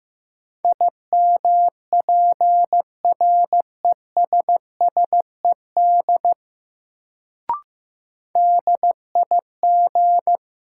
Morse code, heard as a keyed tone, about 15 wpm, 700 Hz. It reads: IMPRESSED DIG
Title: Sentences from Top 1600 Words - New Word Drill - 15wpm